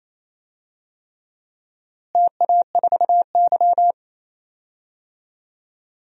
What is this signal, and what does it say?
Morse code, heard as a keyed tone, about 28 wpm, 700 Hz. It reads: TA4Y
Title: DXCC-100 Call Signs - 4 Characters - Repeated 3x - 28wpm